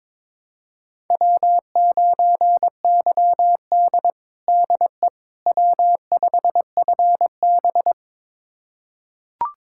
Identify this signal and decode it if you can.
Morse code, heard as a keyed tone, about 22 wpm, 700 Hz. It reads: W9YD DE W5FB
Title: USA Callsign DE Callsign - 4 Characters - 22wpm